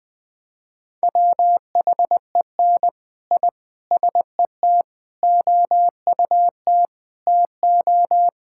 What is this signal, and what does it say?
Morse code, heard as a keyed tone, about 21 wpm, 700 Hz. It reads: WHEN I SET OUT TO
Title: Sentences from Top 1000 Words - Review Part 6 - 20wpm